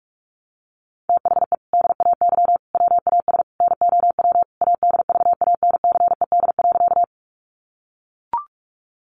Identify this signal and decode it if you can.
Morse code, heard as a keyed tone, about 44 wpm, 700 Hz. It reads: THE DAY WAS NOW ADVANCED.
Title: Sentences from Top 1200 Words 45wpm